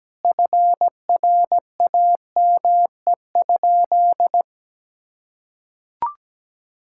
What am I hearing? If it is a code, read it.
Morse code, heard as a keyed tone, about 17 wpm, 700 Hz. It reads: FRAME?
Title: Sentences from Top 1000 Words - Review Part 6 - 17wpm